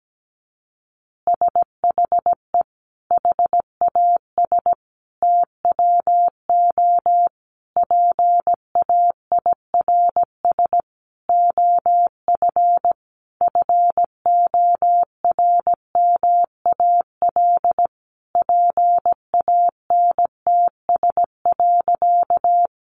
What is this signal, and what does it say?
Morse code, heard as a keyed tone, about 17 wpm, 700 Hz. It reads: SHE HAS TWO PAIRS OF FORMAL PANTS.
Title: Additional Easy Phrases 17wpm